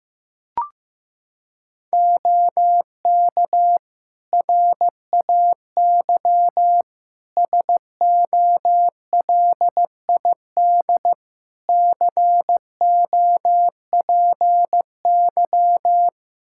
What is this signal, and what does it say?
Morse code, heard as a keyed tone, about 15 wpm, 700 Hz. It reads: OK RAY SOLID COPY
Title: Practice QSO - Exchange 3 - More Info - 15wpm